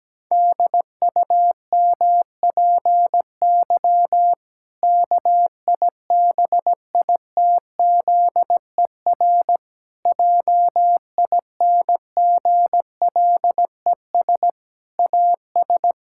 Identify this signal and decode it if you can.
Morse code, heard as a keyed tone, about 17 wpm, 700 Hz. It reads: DUMPY KIBITZER JINGLES AS
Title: Pangram Challenge v2 - 17wpm